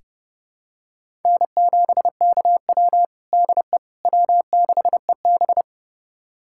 Morse code, heard as a keyed tone, about 30 wpm, 700 Hz. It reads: N7KW DE W6EB
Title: USA Callsign DE Callsign - 4 Characters - 30wpm